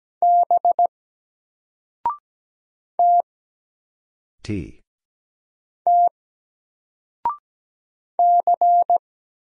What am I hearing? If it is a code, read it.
Morse code, heard as a keyed tone, about 17 wpm, 700 Hz. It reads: B T T C
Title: Single Letter-Number 17wpm